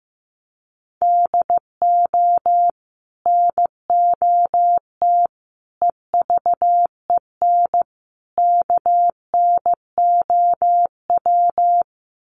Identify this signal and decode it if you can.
Morse code, heard as a keyed tone, about 15 wpm, 700 Hz. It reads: DO NOT EVEN KNOW
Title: Sets of 4 Words 15wpm